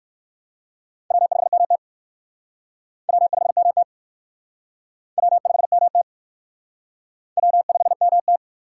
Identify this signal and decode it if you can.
Morse code, heard as a keyed tone, about 58 wpm, 700 Hz. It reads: W5MT W5MT W5MT W5MT